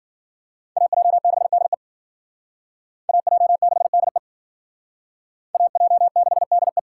Morse code, heard as a keyed tone, about 52 wpm, 700 Hz. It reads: AJ6DE AJ6DE AJ6DE